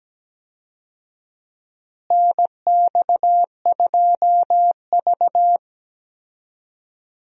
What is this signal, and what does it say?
Morse code, heard as a keyed tone, about 17 wpm, 700 Hz. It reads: NX2V
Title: USA Call Signs - 4 Characters - Repeated 3x - 17wpm